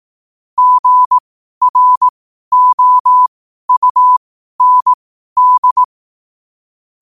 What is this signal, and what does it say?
Morse code, heard as a keyed tone, about 18 wpm, spaced as for 12 wpm, 1000 Hz. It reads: GROUND
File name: qotdfiles / 12wpm